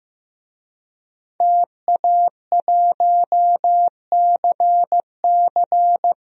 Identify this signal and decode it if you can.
Morse code, heard as a keyed tone, about 15 wpm, 700 Hz. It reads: TA1CC
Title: DXCC-100 Call Signs - 5 Characters - 15wpm